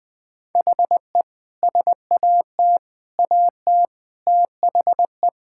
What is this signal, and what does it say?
Morse code, heard as a keyed tone, about 19 wpm, 700 Hz. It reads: HE SAT AT THE
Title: Sentences with 5-Word Phrases - 20wpm